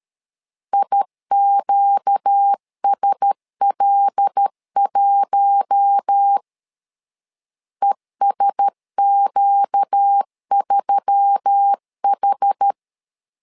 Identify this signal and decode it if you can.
Morse code, heard as a keyed tone, about 13 wpm, 800 Hz. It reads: IQSL1 ESQ3H
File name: LEZIONI / 5 LBVC / 5 L 12wpm 1913